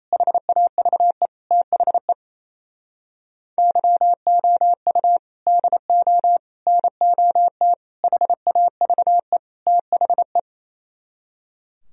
Morse code, heard as a keyed tone, about 32 wpm, 700 Hz. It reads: HAVE THE YOU DO NOT HAVE THE
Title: Speed-Racing - Sets of 5 Words 28wpm